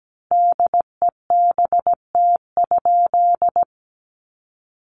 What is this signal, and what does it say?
Morse code, heard as a keyed tone, about 17 wpm, 700 Hz. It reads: DEBT?
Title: Sentences from Top 1300 Words 17wpm